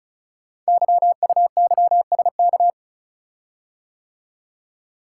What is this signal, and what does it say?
Morse code, heard as a keyed tone, about 36 wpm, 690 Hz. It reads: YUYSK
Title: Code Groups - 5 Letters - 35wpm